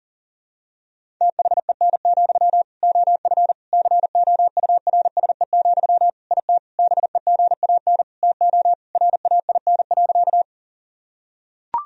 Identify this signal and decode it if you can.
Morse code, heard as a keyed tone, about 37 wpm, 700 Hz. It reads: THEN, OF COURSE, IT BEGAN TO RAIN.
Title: Sentences from Top 1000 Words - Review Part 1 - 40wpm